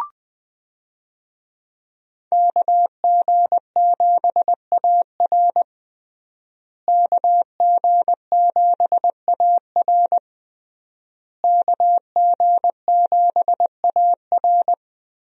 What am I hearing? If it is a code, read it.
Morse code, heard as a keyed tone, about 20 wpm, 700 Hz. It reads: KG7AR KG7AR KG7AR
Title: USA Call Signs - 5 Characters - Repeated 3x - 20wpm